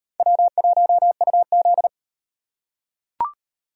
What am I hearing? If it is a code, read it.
Morse code, heard as a keyed tone, about 35 wpm, 700 Hz. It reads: W1UZ